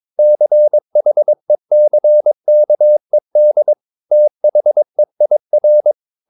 Morse code, heard as a keyed tone, about 22 wpm, 600 Hz. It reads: CHECKED THEIR
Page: Romanen Never af Ken Follet på engelsk.